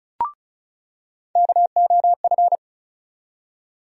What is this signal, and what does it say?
Morse code, heard as a keyed tone, about 36 wpm, 700 Hz. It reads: KOF
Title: USA Call Sign Postfixes - 3 Letters 35wpm